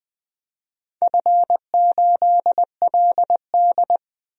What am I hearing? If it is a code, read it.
Morse code, heard as a keyed tone, about 20 wpm, 700 Hz. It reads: F8LD
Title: DXCC-100 Call Signs - 4 Characters - Repeated 3x - 20wpm